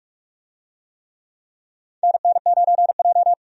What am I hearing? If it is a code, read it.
Morse code, heard as a keyed tone, about 46 wpm, 700 Hz. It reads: NN9J
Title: USA Call Signs - 4 Characters - 45wpm